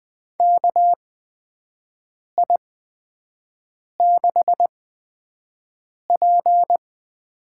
Morse code, heard as a keyed tone, about 20 wpm, spaced as for 5 wpm, 700 Hz. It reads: KI6P
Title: USA Call Signs - 6 Characters - 8x Character Spacing - 20wpm